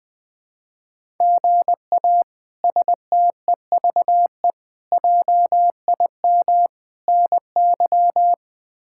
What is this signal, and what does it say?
Morse code, heard as a keyed tone, about 20 wpm, 700 Hz. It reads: GA STEVE JIM NY
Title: K1USN SST QSO Practice - 20wpm